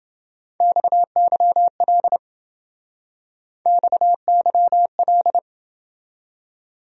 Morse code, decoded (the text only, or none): XYL XYL